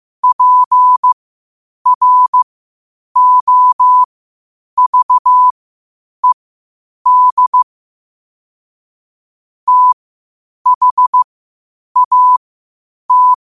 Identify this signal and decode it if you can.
Morse code, heard as a keyed tone, about 15 wpm, spaced as for 8 wpm, 1000 Hz. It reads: PROVED THAT
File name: qotdfiles / 8wpm